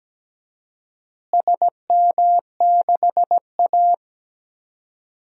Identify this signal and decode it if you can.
Morse code, heard as a keyed tone, about 17 wpm, 700 Hz. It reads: SM6A
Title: DXCC-100 Call Signs - 4 Characters - Repeated 3x - 17wpm